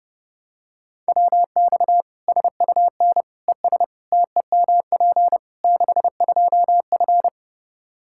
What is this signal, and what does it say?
Morse code, heard as a keyed tone, about 30 wpm, 700 Hz. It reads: WX SUN ES TEMP 62F
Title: Practice QSO - Exchange 2 - Info - 30wpm